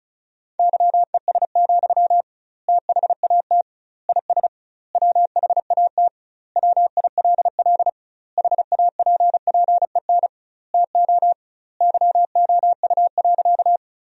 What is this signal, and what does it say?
Morse code, heard as a keyed tone, about 36 wpm, 700 Hz. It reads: YES, THAT IS WHAT WILL HAPPEN TO YOU.
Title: 2x Word-Spacing - Sentences from Top 500 Words - 35wpm